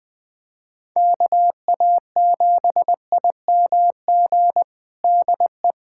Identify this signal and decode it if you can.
Morse code, heard as a keyed tone, about 21 wpm, 700 Hz. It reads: KA7IMG DE
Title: USA Callsign DE Callsign - 6 Characters - 20wpm